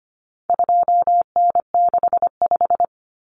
Morse code, heard as a keyed tone, about 25 wpm, 700 Hz. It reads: 2N65
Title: Common Cut Numbers - 4-Digit 25wpm